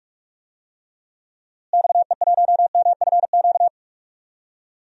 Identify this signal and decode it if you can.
Morse code, heard as a keyed tone, about 46 wpm, 700 Hz. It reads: XE1MFQ
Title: DXCC-100 Call Signs - 6 Characters - Repeated 2x - 45wpm